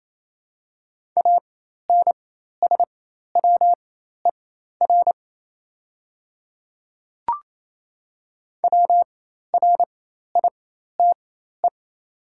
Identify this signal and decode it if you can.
Morse code, heard as a keyed tone, about 27 wpm, spaced as for 13 wpm, 690 Hz. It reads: ANSWER WRITE
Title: 4x Character Spacing - Top 300 Words - 28wpm